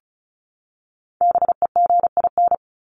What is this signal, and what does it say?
Morse code, heard as a keyed tone, about 34 wpm, 700 Hz. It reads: BEGIN
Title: Top 300 Words 35wpm